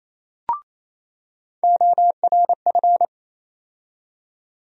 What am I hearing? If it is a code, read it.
Morse code, heard as a keyed tone, about 28 wpm, 700 Hz. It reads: ORF
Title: USA Call Sign Postfixes - 3 Letters 28wpm